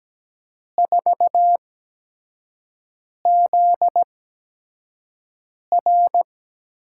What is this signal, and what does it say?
Morse code, heard as a keyed tone, about 17 wpm, spaced as for 5 wpm, 700 Hz. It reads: 4ZR
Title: USA Call Signs - 8x Character Spacing - 17wpm